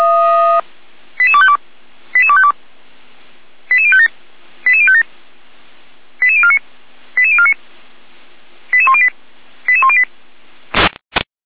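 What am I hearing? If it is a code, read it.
Five-tone selective call (ZVEI), heard as quick runs of five tones, tones of about 70 ms. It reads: 88252 88252 88057 88057 88049 88049 88108 88108
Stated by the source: ZVEI five-tone call 88191 received by SDR.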